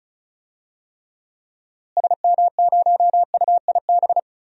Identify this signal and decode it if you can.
Morse code, heard as a keyed tone, about 36 wpm, 700 Hz. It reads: SM0UIB
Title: DXCC-100 Call Signs - 6 Characters - Repeated 3x - 35wpm